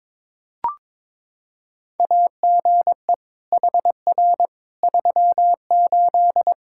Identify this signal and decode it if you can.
Morse code, heard as a keyed tone, about 22 wpm, 700 Hz. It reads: AGE HR 38
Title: Practice QSO - Exchange 3 - More Info - 22wpm